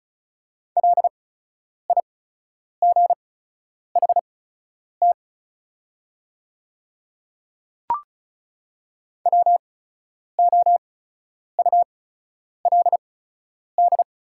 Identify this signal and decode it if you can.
Morse code, heard as a keyed tone, about 36 wpm, spaced as for 10 wpm, 690 Hz. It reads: LIGHT WOULD
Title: 8x Character Spacing - Top 200 Words - 35wpm